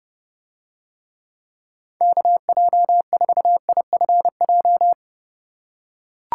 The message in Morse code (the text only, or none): KJ4IFJ